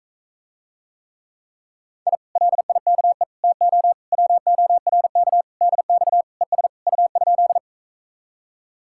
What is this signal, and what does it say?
Morse code, heard as a keyed tone, about 46 wpm, 700 Hz. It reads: I LIKE TO WORK DX ES U?